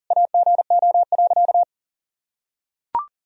Morse code, heard as a keyed tone, about 37 wpm, 700 Hz. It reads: AGO.
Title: Sentences from Top 1000 Words - Review Part 3 - 40wpm